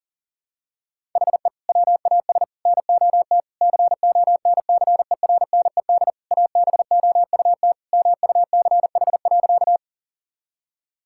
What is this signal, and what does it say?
Morse code, heard as a keyed tone, about 40 wpm, 700 Hz. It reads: HE WAS NOT CONCERNED ABOUT MUCH.
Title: Sentences from Top 1000 Words - Review Part 6 - 40wpm